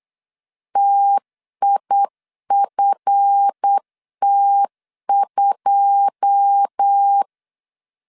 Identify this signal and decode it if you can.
Morse code, heard as a keyed tone, about 8 wpm, 800 Hz. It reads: TIFT2
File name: CW / DET / 1 9 1500C S C 8wpm 5653